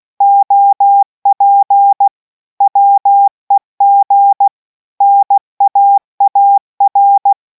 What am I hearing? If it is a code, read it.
Morse code, heard as a keyed tone, about 16 wpm, 800 Hz. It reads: OP WEG NAAR